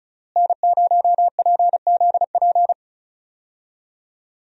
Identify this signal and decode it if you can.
Morse code, heard as a keyed tone, about 36 wpm, 700 Hz. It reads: N0PZP
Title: USA Call Signs - 5 Characters - Repeated 3x - 35wpm